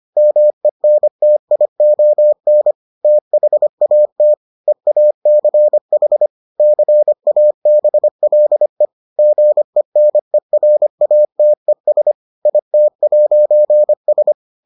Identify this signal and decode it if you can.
Morse code, heard as a keyed tone, about 25 wpm, 600 Hz. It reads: MENTION THAT EACH CABLE GENERATES IT'S